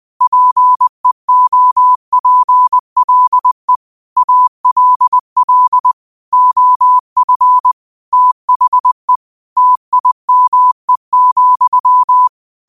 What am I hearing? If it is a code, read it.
Morse code, heard as a keyed tone, about 20 wpm, 1000 Hz. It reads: PEOPLE ALL OF THE TIME,
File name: qotdfiles / 20wpm